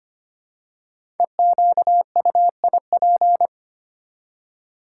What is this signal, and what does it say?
Morse code, heard as a keyed tone, about 25 wpm, 700 Hz. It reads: EQUIP
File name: 5 Letter Words - First 3 Letters Given - 25wpm